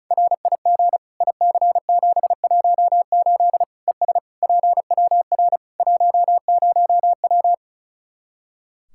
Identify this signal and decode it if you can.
Morse code, heard as a keyed tone, about 36 wpm, 700 Hz. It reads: RIG IC718 ES PWR 10W
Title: Practice QSO - Exchange 2 - Info - 35wpm